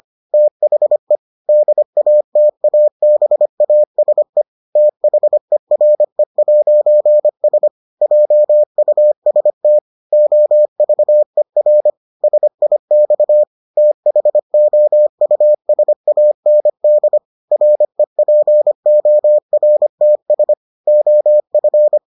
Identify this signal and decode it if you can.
Morse code, heard as a keyed tone, about 25 wpm, 600 Hz. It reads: THE DATABASE THERE'S JUST OVER SIX THOUSAND REPORTS OF